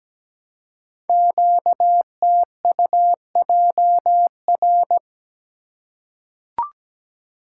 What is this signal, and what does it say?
Morse code, heard as a keyed tone, about 17 wpm, 700 Hz. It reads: QTUJR
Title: Code Groups - 5 Letters - 17wpm